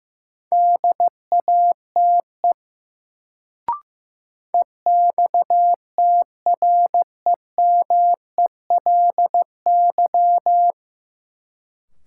Morse code, heard as a keyed tone, about 15 wpm, 700 Hz. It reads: DATE EXTREMELY
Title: Sentences from Top 1000 Words - Review Part 7 - New Word Drill - 15wpm